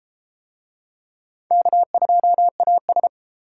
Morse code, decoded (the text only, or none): K2AS